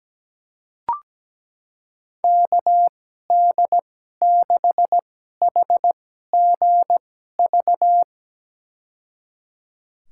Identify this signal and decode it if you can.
Morse code, heard as a keyed tone, about 17 wpm, spaced as for 12 wpm, 700 Hz. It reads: KD6HGV
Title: USA Call Signs - 6 Characters - 2x Character Spacing - 17wpm